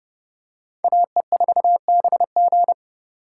Morse code, heard as a keyed tone, about 29 wpm, 700 Hz. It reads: AE4BG